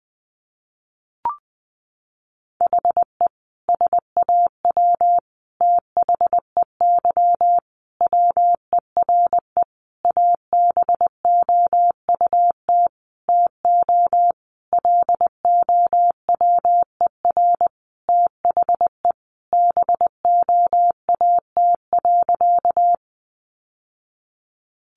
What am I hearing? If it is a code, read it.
Morse code, heard as a keyed tone, about 20 wpm, 700 Hz. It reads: HE SAW THEY WERE ABOUT TO LOWER THE BOAT.
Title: Sentences from Top 700 Words 20wpm